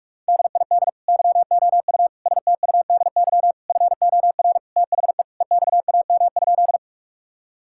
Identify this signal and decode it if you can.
Morse code, heard as a keyed tone, about 46 wpm, 700 Hz. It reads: DID YOU STUDY FOR THE EXAM?
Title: Sentences from Top 1000 Words - Review Part 5 - 45wpm